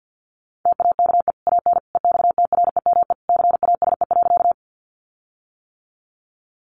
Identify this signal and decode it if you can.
Morse code, heard as a keyed tone, about 48 wpm, 700 Hz. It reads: TAKE AN EXTREME CASE.
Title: Sentences from Top 1300 Words 50wpm